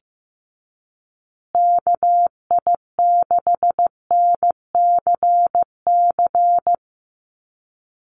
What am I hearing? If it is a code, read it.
Morse code, heard as a keyed tone, about 15 wpm, 700 Hz. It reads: KI6NCC
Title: USA Call Signs Repeated 3x 15wpm